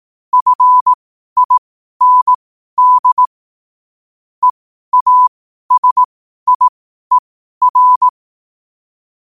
Morse code, heard as a keyed tone, about 18 wpm, spaced as for 12 wpm, 1000 Hz. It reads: FIND EASIER
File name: qotdfiles / 12wpm